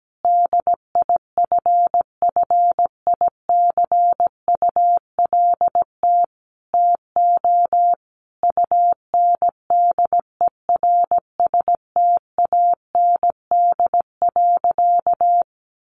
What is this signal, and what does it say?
Morse code, heard as a keyed tone, about 17 wpm, 700 Hz. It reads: DIFFICULT TO UNDERSTAND.
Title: Sentences from Top 1100 Words 17wpm